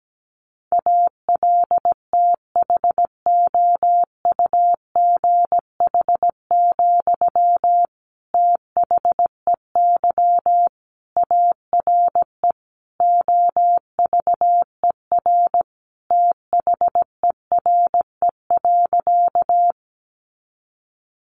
Morse code, heard as a keyed tone, about 17 wpm, 700 Hz. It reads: ALTHOUGH, THEY ARE OVER THERE.
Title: Sentences from Top 600 Words 17wpm